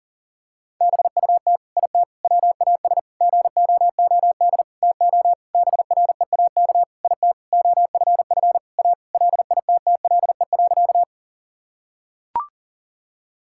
Morse code, heard as a keyed tone, about 40 wpm, 700 Hz. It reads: BUT IT WAS GOOD TO BREAK IT OFF A LITTLE.
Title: Sentences from Top 1000 Words - Review Part 6 - 40wpm